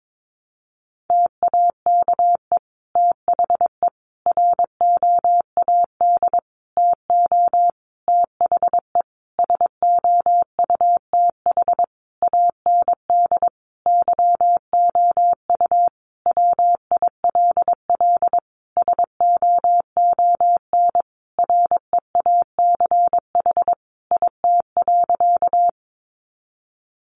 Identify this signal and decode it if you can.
Morse code, heard as a keyed tone, about 22 wpm, 700 Hz. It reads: TAKE THE ROAD TO THE SOUTH AND YOU WILL SOON REACH IT.
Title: Sentences from Top 400 Words 22wpm